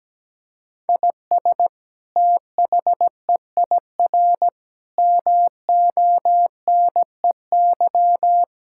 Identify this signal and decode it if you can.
Morse code, heard as a keyed tone, about 17 wpm, 700 Hz. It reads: IS THEIR MONEY
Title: Sentences from Top 1600 Words 17wpm